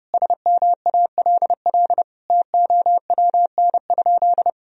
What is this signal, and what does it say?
Morse code, heard as a keyed tone, about 30 wpm, 700 Hz. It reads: SMALL TOWN?
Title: Sentences from Top 1000 Words - Review Part 4 - 30wpm